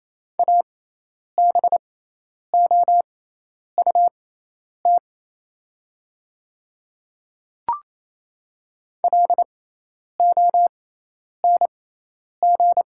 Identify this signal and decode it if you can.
Morse code, heard as a keyed tone, about 28 wpm, spaced as for 10 wpm, 700 Hz. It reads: ABOUT LONG
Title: 6x Character Spacing - Top 100 Words - 28wpm